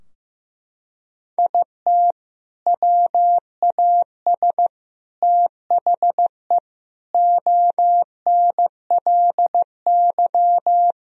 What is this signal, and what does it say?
Morse code, heard as a keyed tone, about 16 wpm, 700 Hz. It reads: IT WAS THE ONLY
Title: Sentences from Top 1000 Words - Review Part 3 - 15wpm